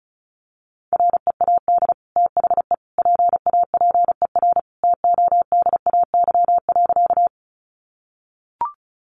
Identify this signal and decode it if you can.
Morse code, heard as a keyed tone, about 34 wpm, 700 Hz. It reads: READ THE PAPER TODAY.
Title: Sentences from Top 600 Words 35wpm